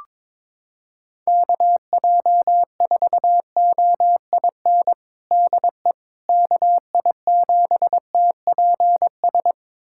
Morse code, heard as a keyed tone, about 22 wpm, 700 Hz. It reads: KJ4OIN DE KI7TPS
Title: USA Callsign DE Callsign 22wpm